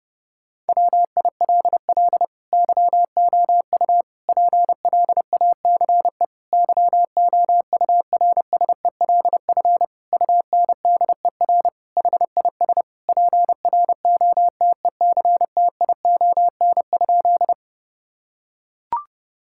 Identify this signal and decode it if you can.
Morse code, heard as a keyed tone, about 32 wpm, 700 Hz. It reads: WILL YOU PLACE YOURSELF UNDER HIS PROTECTION?
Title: Sentences from Top 1600 Words 30wpm